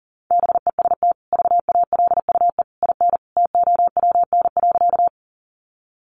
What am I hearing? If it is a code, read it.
Morse code, heard as a keyed tone, about 40 wpm, 700 Hz. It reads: BEST VALUE IN TOWN.
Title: Sentences from Top 600 Words 40wpm